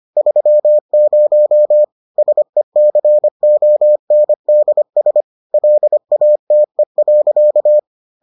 Morse code, heard as a keyed tone, about 25 wpm, 600 Hz. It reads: 30 SECONDS LATE.